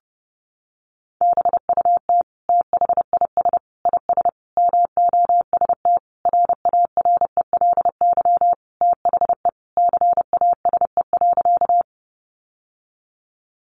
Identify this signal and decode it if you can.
Morse code, heard as a keyed tone, about 30 wpm, 700 Hz. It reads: BUT THIS IS MOST RARELY THE CASE.
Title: Sentences from Top 1400 Words 30wpm